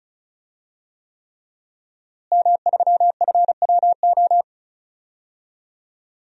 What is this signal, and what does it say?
Morse code, heard as a keyed tone, about 34 wpm, 700 Hz. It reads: M3FWO
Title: DXCC-100 Call Signs - 5 Characters - Repeated 3x - 35wpm